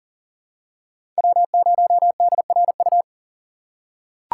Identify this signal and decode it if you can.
Morse code, heard as a keyed tone, about 40 wpm, 700 Hz. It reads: W0DRU
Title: USA Call Signs - 5 Characters - 3x Character Spacing - 40wpm